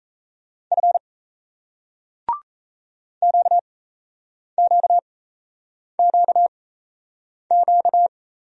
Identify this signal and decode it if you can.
Morse code, heard as a keyed tone, about 37 wpm, spaced as for 8 wpm, 700 Hz. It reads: F QQQQ